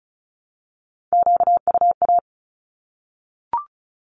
Morse code, heard as a keyed tone, about 34 wpm, 700 Hz. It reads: QUA
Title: Q-Codes 35wpm